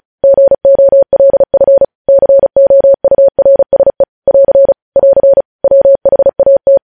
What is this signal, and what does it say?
Morse code, heard as a keyed tone, about 36 wpm, 550 Hz. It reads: GOLF COURSE <AR> <AR> WHAT
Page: News Headlines in Morse Code at 35 WPM